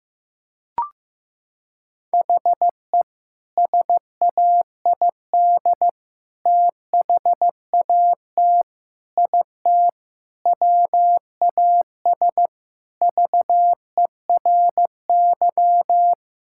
Morse code, heard as a keyed tone, about 15 wpm, 700 Hz. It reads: HE SAID THAT IT WAS VERY
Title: Sentences from Top 1000 Words - Review Part 6 - 15wpm